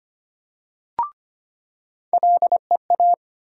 Morse code, heard as a keyed tone, about 25 wpm, 700 Hz. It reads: LEA
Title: USA Call Sign Postfixes - 3 Letters 25wpm